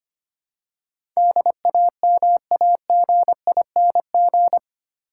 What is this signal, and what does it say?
Morse code, heard as a keyed tone, about 25 wpm, 700 Hz. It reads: DAMAGING
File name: 8 Letter Words - First 3 Letters Given - 25wpm